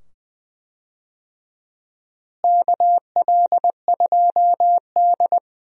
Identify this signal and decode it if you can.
Morse code, heard as a keyed tone, about 20 wpm, 700 Hz. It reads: KL2D
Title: USA Call Signs - 4 Characters - Repeated 2x - 20wpm